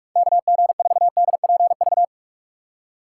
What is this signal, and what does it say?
Morse code, heard as a keyed tone, about 46 wpm, 700 Hz. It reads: KG4DPV
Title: USA Call Signs - 6 Characters - 6x Character Spacing - 45wpm